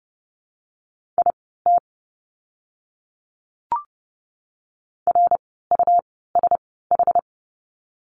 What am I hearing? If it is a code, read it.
Morse code, heard as a keyed tone, about 30 wpm, spaced as for 17 wpm, 700 Hz. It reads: IT RUSH
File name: Lesson 17f - Letter u - Words 30wpm